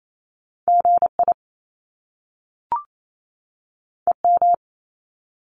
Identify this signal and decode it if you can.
Morse code, heard as a keyed tone, about 28 wpm, 700 Hz. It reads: GI EM
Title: Top Two Letter Combinations in Words 28wpm